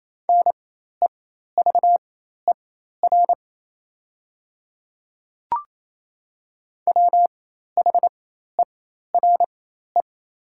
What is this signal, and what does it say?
Morse code, heard as a keyed tone, about 28 wpm, spaced as for 13 wpm, 700 Hz. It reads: NEVER WHERE
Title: 4x Character Spacing - Top 300 Words - 28wpm